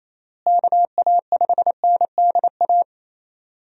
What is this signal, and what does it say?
Morse code, heard as a keyed tone, about 28 wpm, 700 Hz. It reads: KA5NDA
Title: USA Call Signs - 6 Characters - Repeated 3x - 28wpm